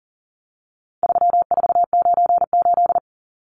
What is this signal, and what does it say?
Morse code, heard as a keyed tone, about 40 wpm, 700 Hz. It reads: 3498
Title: Numbers - 4-Digit 40wpm